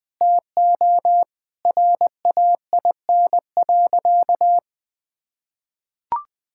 Morse code, heard as a keyed tone, about 20 wpm, 700 Hz. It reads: TO RAIN.
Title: Sentences with 4-Word Phrases - 20wpm